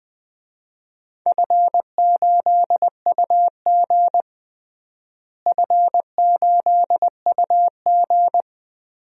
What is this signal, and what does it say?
Morse code, heard as a keyed tone, about 20 wpm, 700 Hz. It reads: F8UG F8UG
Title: DXCC-100 Call Signs - 4 Characters - Repeated 2x - 20wpm